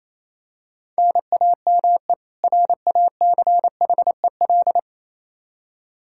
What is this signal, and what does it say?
Morse code, heard as a keyed tone, about 28 wpm, 700 Hz. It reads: NAME RACHEL
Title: Practice QSO - Exchange 1 - Beginning - 28wpm